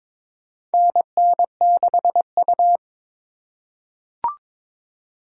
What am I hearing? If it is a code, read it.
Morse code, heard as a keyed tone, about 22 wpm, 700 Hz. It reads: NN6U
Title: USA Call Signs - 4 Characters - 3x Character Spacing - 22wpm